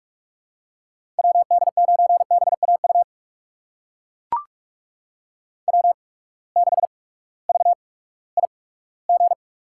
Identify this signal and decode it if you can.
Morse code, heard as a keyed tone, about 44 wpm, spaced as for 20 wpm, 700 Hz. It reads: WD9BAU W6VIG
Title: USA Call Signs - 8x Character Spacing - 45wpm